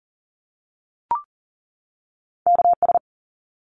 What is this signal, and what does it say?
Morse code, heard as a keyed tone, about 40 wpm, 700 Hz. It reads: KS
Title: ARRL Sections 40wpm